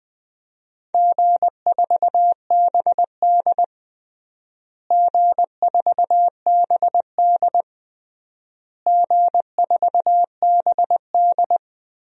Morse code, heard as a keyed tone, about 20 wpm, 700 Hz. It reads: G4BD G4BD G4BD
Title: DXCC-100 Call Signs - 4 Characters - Repeated 3x - 20wpm